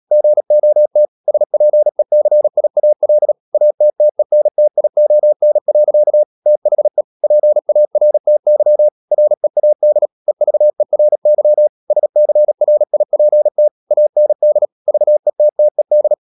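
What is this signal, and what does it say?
Morse code, heard as a keyed tone, about 37 wpm, 600 Hz. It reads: GOT SPECIAL ATTENTION. THE PARTY READ EVERY SCRIPT AND VETTED
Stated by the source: Morse code practice at 35–39 wpm sending English text of Ken Follett's novel Never, from CW med Gnister.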